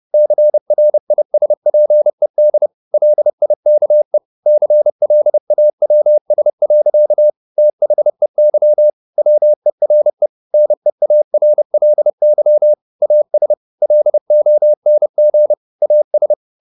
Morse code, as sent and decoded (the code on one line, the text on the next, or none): -.-. .-. .. ... .--. -.. / .-.. .. -.- . / -.-. .-.. .- .-- ... .-.-.- / - .... -.-- / .-- . .-. / -. .- .-. .-.. -.-- / .- ... / .-.. --- -. --. / .- ...
CRISPED LIKE CLAWS. THEY WERE NEARLY AS LONG AS